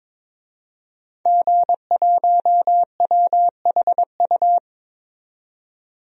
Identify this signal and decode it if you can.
Morse code, heard as a keyed tone, about 22 wpm, 700 Hz. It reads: G1WHU
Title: DXCC-100 Call Signs - 5 Characters - 22wpm